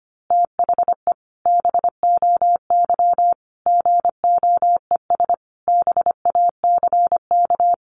Morse code, heard as a keyed tone, about 25 wpm, 700 Hz. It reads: THE BOY GOES BACK
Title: Easy Phrases 25wpm